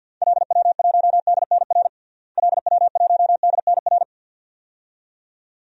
Morse code, heard as a keyed tone, about 50 wpm, 700 Hz. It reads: LW1DNR LW1DNR
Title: DXCC-100 Call Signs - 6 Characters - Repeated 2x - 50wpm